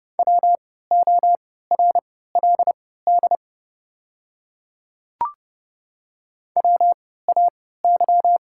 Morse code, heard as a keyed tone, about 31 wpm, spaced as for 17 wpm, 700 Hz. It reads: WORLD WAY
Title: 3x Character Spacing - Top 300 Words - 30wpm